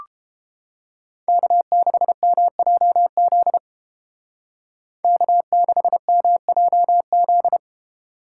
Morse code, heard as a keyed tone, about 31 wpm, 700 Hz. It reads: K6MJZ K6MJZ